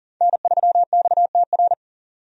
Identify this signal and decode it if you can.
Morse code, heard as a keyed tone, about 40 wpm, 700 Hz. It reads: N3XTR
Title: USA Call Signs - 5 Characters - 40wpm